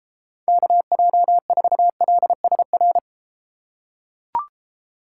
Morse code, heard as a keyed tone, about 31 wpm, 700 Hz. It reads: KJ4LSR